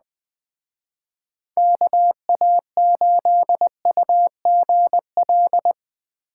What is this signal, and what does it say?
Morse code, heard as a keyed tone, about 20 wpm, 700 Hz. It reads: KA8UGL